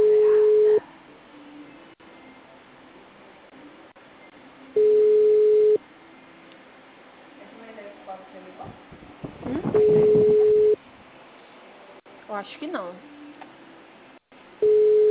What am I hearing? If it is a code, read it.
Ringback tone (Europe, 425 Hz), with speech.
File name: out- -2062- - - .88697